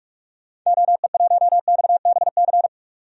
Morse code, heard as a keyed tone, about 44 wpm, 700 Hz. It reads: OE1XBC
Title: DXCC-100 Call Signs - 6 Characters - 45wpm